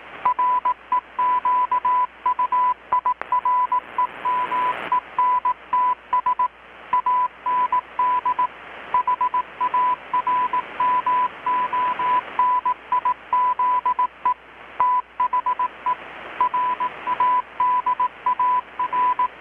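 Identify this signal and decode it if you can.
Morse code, heard as a keyed tone, about 18 wpm, 1000 Hz, with radio noise. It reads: REQUIREMENTS AND HARMONIZE THE RADAR
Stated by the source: Morse from KPH's Night of Nights XX maritime radiotelegraphy event, received in Chiapas, Mexico about 3,800 km away on a Kenwood R-600 with a grounded 10 m long-wire antenna.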